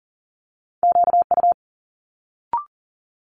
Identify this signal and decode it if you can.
Morse code, heard as a keyed tone, about 40 wpm, 700 Hz. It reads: QU
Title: Multi-letter Phonemes Quick Fire 40wpm